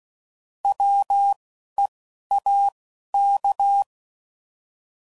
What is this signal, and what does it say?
Morse code, heard as a keyed tone, about 16 wpm, spaced as for 11 wpm, 800 Hz. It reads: WEAK
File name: Prefixes / By First Letter / Ws